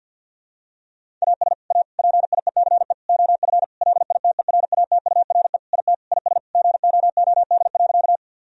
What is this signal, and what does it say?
Morse code, heard as a keyed tone, about 52 wpm, 700 Hz. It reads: AS A PIECE OF LITERATURE IT IS GOOD.
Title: Sentences from Top 1000 Words - Review Part 3 - 50wpm